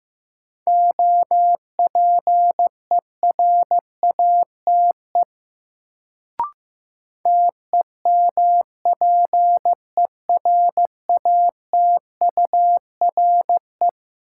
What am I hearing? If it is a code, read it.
Morse code, heard as a keyed tone, about 15 wpm, 700 Hz. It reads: OPERATE TEMPERATURE
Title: Sentences from Top 1500 Words - New Word Drill Alt - 15wpm